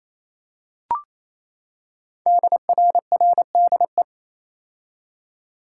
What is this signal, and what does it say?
Morse code, heard as a keyed tone, about 28 wpm, 700 Hz. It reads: DRRDE
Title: Code Groups - 5 Letters - 28wpm